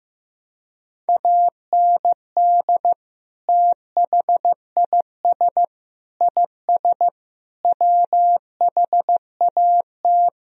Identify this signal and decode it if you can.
Morse code, heard as a keyed tone, about 15 wpm, 700 Hz. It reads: AND THIS IS WHAT
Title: Sentences from Top 1000 Words - Review Part 3 - 15wpm